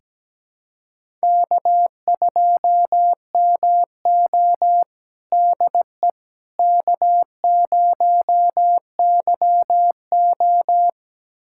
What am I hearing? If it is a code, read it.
Morse code, heard as a keyed tone, about 17 wpm, 700 Hz. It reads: K2MO DE K0YO
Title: USA Callsign DE Callsign - 4 Characters - 17wpm